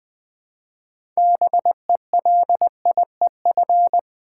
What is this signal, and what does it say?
Morse code, heard as a keyed tone, about 20 wpm, 690 Hz. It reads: BELIEF
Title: Build A Word - 2 Letters At A Time - 6-Letter Words - 20wpm